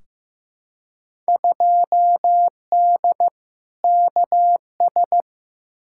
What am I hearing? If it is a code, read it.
Morse code, heard as a keyed tone, about 15 wpm, 700 Hz. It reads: 2D KS
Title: Field Day Exchanges Encore - 15wpm